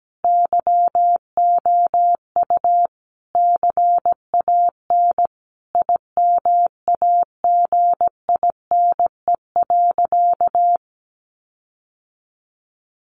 Morse code, heard as a keyed tone, about 17 wpm, 700 Hz. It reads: YOU CAN IMAGINE.
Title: Sentences from Top 1400 Words 17wpm